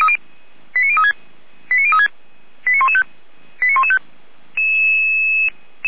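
Five-tone selective call (ZVEI), heard as quick runs of five tones, tones of about 70 ms. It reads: 89036 89036 89115 89115
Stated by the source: ZVEI five-tone selective call received on an SDR.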